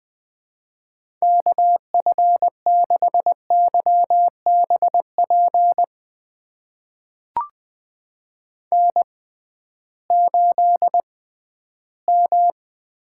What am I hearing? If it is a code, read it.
Morse code, heard as a keyed tone, about 20 wpm, spaced as for 13 wpm, 700 Hz. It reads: KF6YBP N8M
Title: USA Call Signs - 6x Character Spacing - 20wpm